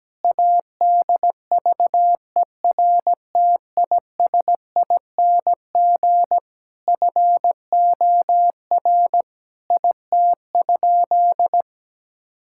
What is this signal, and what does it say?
Morse code, heard as a keyed tone, about 17 wpm, 700 Hz. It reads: ADVERTISING FOR IT?
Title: Sentences from Top 1000 Words - Review Part 6 - 17wpm